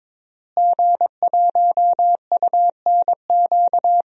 Morse code, heard as a keyed tone, about 22 wpm, 700 Hz. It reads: G1UNQ
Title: DXCC-100 Call Signs - 5 Characters - Repeated 3x - 22wpm